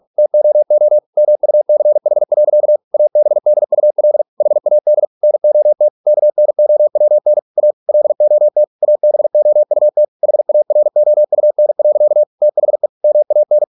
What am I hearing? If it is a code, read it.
Morse code, heard as a keyed tone, about 46 wpm, 600 Hz. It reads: TOO MUCH. ABDUL HAD NOT KNOWN A LOT ABOUT HAROUN. THE MAN